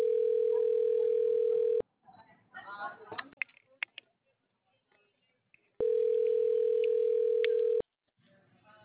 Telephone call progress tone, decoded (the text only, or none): ringback tone